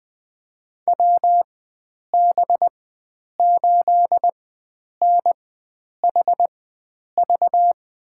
Morse code, heard as a keyed tone, about 20 wpm, spaced as for 9 wpm, 700 Hz. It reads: WB8NHV
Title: USA Call Signs - 6 Characters - 4x Character Spacing - 20wpm